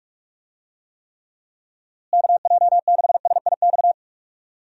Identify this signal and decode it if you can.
Morse code, heard as a keyed tone, about 46 wpm, 700 Hz. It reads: KJ6SIX
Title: USA Call Signs - 6 Characters - Repeated 2x - 45wpm